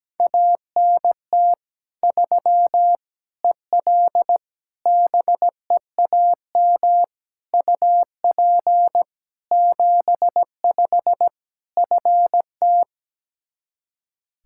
Morse code, heard as a keyed tone, about 17 wpm, 700 Hz. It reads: ANT 3 EL BEAM UP 75 FT
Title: Practice QSO - Exchange 2 - Info - 17wpm